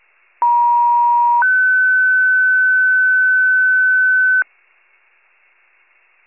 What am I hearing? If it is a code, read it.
Two-tone page, heard as a one-second tone, then a three-second tone, about 950 then 1500 Hz. It is heard from 0.4 to 4.4 s.